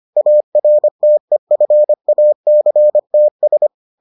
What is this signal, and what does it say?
Morse code, heard as a keyed tone, about 25 wpm, 600 Hz. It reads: ARTEFACTS